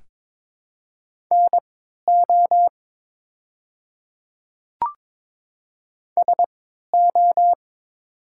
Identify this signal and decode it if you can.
Morse code, heard as a keyed tone, about 23 wpm, spaced as for 13 wpm, 700 Hz. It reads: NO SO
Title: 3x Character Spacing - Top 100 Words - 22wpm